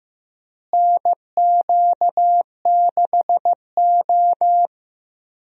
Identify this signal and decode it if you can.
Morse code, heard as a keyed tone, about 15 wpm, 700 Hz. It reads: NQ6O
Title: USA Call Signs - 4 Characters - Repeated 3x - 15wpm